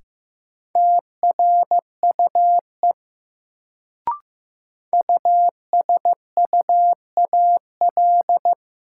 Morse code, heard as a keyed tone, about 15 wpm, 700 Hz. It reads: TRUE USUAL
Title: Sentences from Top 400 Words - New Word Drill - 15wpm